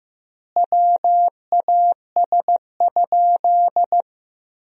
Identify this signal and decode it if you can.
Morse code, heard as a keyed tone, about 15 wpm, 700 Hz. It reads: WAS?
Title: 4x Word-Spacing - Sentences from Top 200 Words - 15wpm